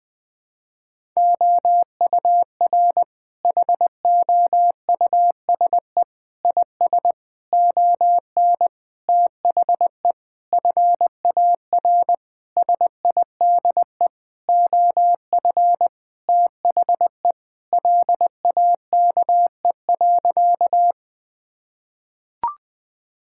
Morse code, heard as a keyed tone, about 20 wpm, 700 Hz. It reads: OUR HOUSE IS ON THE FAR SIDE OF THE LAKE.
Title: Sentences with 5-Word Phrases - 20wpm